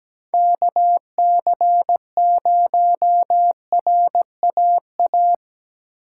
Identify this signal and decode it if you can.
Morse code, heard as a keyed tone, about 20 wpm, 700 Hz. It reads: KC0RAA
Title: Speed Racing - USA 2x3 Call Signs 17wpm